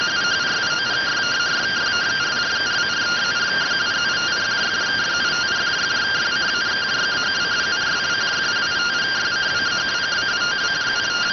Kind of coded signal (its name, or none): two-tone teleprinter signal